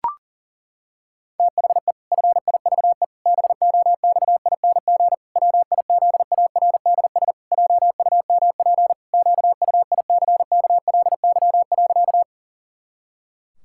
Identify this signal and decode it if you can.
Morse code, heard as a keyed tone, about 40 wpm, 700 Hz. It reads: THE FIVE BOXING WIZARDS JUMP QUICKLY.
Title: Pangram Challenge v2 - 40wpm